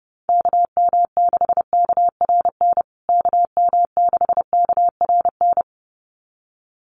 Morse code, heard as a keyed tone, about 30 wpm, 700 Hz. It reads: KM6KRN KM6KRN
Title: USA Call Signs Repeated 2x 30wpm